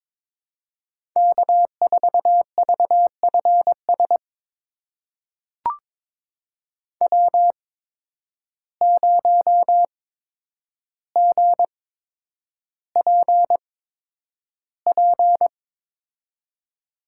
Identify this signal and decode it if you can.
Morse code, heard as a keyed tone, about 22 wpm, spaced as for 9 wpm, 700 Hz. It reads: K4VFS W0GPP
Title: USA Call Signs - 5 Characters - 8x Character Spacing - 22wpm